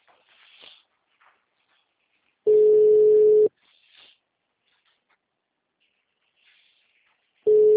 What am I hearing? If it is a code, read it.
Ringback tone (Europe, 425 Hz).